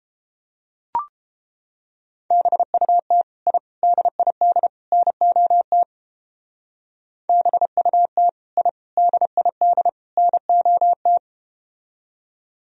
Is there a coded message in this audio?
Yes: Morse code from 2.3 to 11.2 s, heard as a keyed tone, about 33 wpm, 700 Hz. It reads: BUT I DID NOT BUT I DID NOT